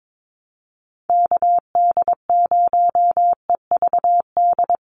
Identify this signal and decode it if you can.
Morse code, heard as a keyed tone, about 22 wpm, 700 Hz. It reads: KD0EVD
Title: USA Call Signs 22wpm